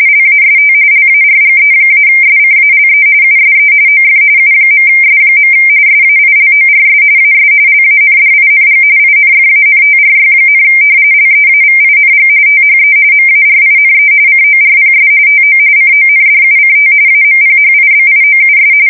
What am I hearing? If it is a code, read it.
RTTY, about 45.45 baud. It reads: LAST LARGE-SCALE USE OF U.S. SAILORS AS LANDING INFANTRY, LEAVING THE MARINE CORPS TO CONDUCT AMPHIBIOUS
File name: fsk170 / usnavy4